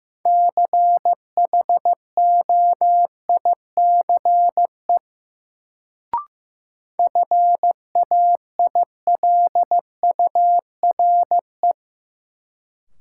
Morse code, heard as a keyed tone, about 15 wpm, 700 Hz. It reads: CHOICE FAILURE
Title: Sentences from Top 900 Words - New Word Drill - 15wpm